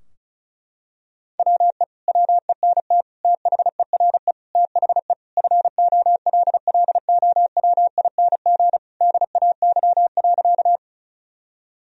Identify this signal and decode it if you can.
Morse code, heard as a keyed tone, about 36 wpm, 700 Hz. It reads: WE WENT THERE THE FOLLOWING DAY.
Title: Sentences from Top 1000 Words - Review Part 4 - 35wpm